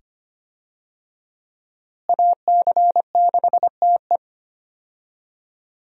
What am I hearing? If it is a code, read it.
Morse code, heard as a keyed tone, about 25 wpm, 700 Hz. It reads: AC6TE
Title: USA Call Signs - 5 Characters - Repeated 2x - 25wpm